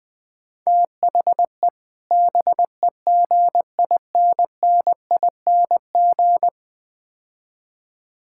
Morse code, heard as a keyed tone, about 20 wpm, 700 Hz. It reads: THE BEGINNING
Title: Build A Phrase - 4-Word Phrases - 20wpm